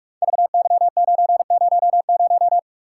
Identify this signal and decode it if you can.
Morse code, heard as a keyed tone, about 44 wpm, 690 Hz. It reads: VY900
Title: Code Groups - 5 Letter:Numbers - 45wpm